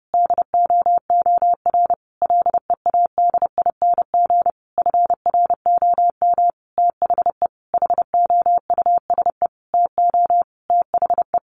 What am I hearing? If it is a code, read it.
Morse code, heard as a keyed tone, about 30 wpm, 700 Hz. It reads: DOOR LEADING FROM THE HOUSE TO THE
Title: Sentences from Top 1200 Words 30wpm